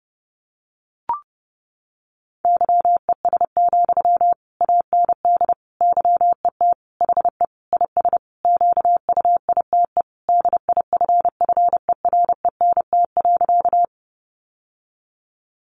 Morse code, heard as a keyed tone, about 30 wpm, 700 Hz. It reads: YES, AND YET HE IS QUITE DIFFERENT.
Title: Sentences from Top 700 Words 30wpm